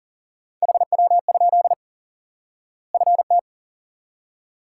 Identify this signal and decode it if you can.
Morse code, heard as a keyed tone, about 40 wpm, 700 Hz. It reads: HW? FT
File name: IQE Course 3 - Lesson 89 - YAGI - Two Elements - 40wpm